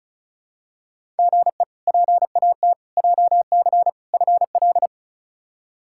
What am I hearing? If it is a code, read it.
Morse code, heard as a keyed tone, about 36 wpm, 700 Hz. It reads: GE PAT JC FL
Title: K1USN SST QSO Practice - 35wpm